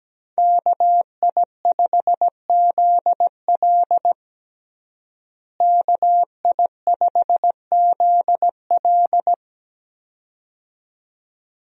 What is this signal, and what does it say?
Morse code, heard as a keyed tone, about 17 wpm, 700 Hz. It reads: KI5ZL KI5ZL
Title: USA Call Signs - 5 Characters - Repeated 2x - 17wpm